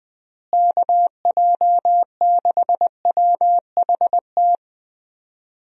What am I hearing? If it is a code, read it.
Morse code, heard as a keyed tone, about 20 wpm, 700 Hz. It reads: KJ6WHT
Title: USA Call Signs - 6 Characters - Repeated 3x - 20wpm